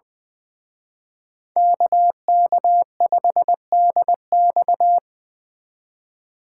Morse code, heard as a keyed tone, about 24 wpm, 700 Hz. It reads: KK5DX
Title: Speed Racing - USA 2x2 Call Signs 20wpm